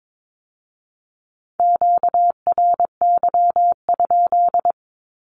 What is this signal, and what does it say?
Morse code, heard as a keyed tone, about 22 wpm, 700 Hz. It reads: QRY?
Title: Q-Codes with Spoken Meaning - 22wpm